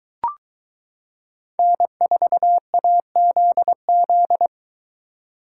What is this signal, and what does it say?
Morse code, heard as a keyed tone, about 23 wpm, 700 Hz. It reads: N4AZZ